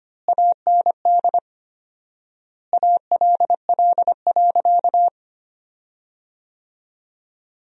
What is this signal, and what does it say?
Morse code, heard as a keyed tone, about 25 wpm, 700 Hz. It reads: AND ALL.
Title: 4x Word-Spacing - Sentences from Top 200 Words - 25wpm